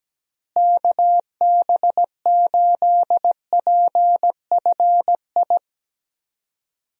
Morse code, heard as a keyed tone, about 17 wpm, 700 Hz. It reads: KB8PFI
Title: USA Call Signs - 6 Characters - Repeated 2x - 17wpm